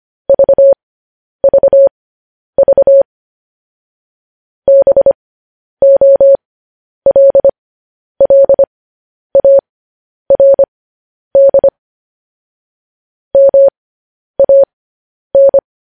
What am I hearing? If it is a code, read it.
Morse code, heard as a keyed tone, about 25 wpm, spaced as for 10 wpm, 550 Hz. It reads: VVV BOLLARD MAN
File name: podcast / News headlines-1x-25x10